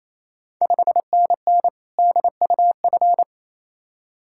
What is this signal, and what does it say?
Morse code, heard as a keyed tone, about 28 wpm, 700 Hz. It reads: 5NN DUF
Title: Ontario QSO Party Exchange 28wpm